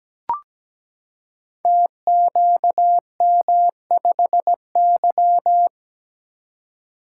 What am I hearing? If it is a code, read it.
Morse code, heard as a keyed tone, about 17 wpm, 700 Hz. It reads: TQM5Y
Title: Code Groups - 5 Letter:Numbers - 17wpm